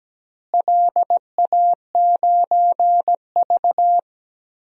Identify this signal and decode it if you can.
Morse code, heard as a keyed tone, about 17 wpm, 700 Hz. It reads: LA9V
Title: DXCC-100 Call Signs - 4 Characters - Repeated 3x - 17wpm